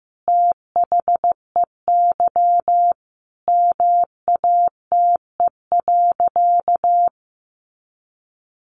Morse code, heard as a keyed tone, about 15 wpm, 700 Hz. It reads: THEY MATE.
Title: A Sound Of Thunder 15wpm